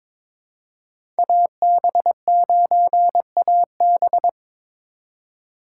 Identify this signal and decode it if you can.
Morse code, heard as a keyed tone, about 22 wpm, 700 Hz. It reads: AB9AB
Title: USA Call Signs - 5 Characters - 4x Character Spacing - 22wpm